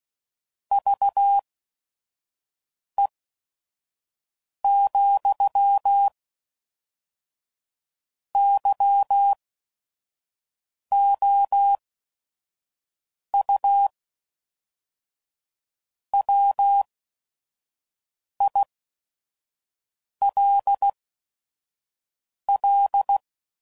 Morse code, heard as a keyed tone, about 16 wpm, spaced as for 5 wpm, 800 Hz. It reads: VE, YOU WILL
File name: CW Code Groups @5WPM